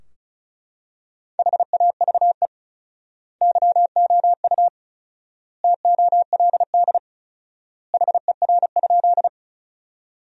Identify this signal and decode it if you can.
Morse code, heard as a keyed tone, about 34 wpm, 700 Hz. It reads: HAVE YOU TOLD HER?
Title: 4x Word-Spacing - Sentences from Top 400 Words - 35wpm